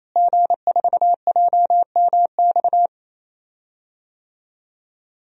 Morse code, heard as a keyed tone, about 28 wpm, 700 Hz. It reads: G4JMX
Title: DXCC-100 Call Signs - 5 Characters - Repeated 3x - 28wpm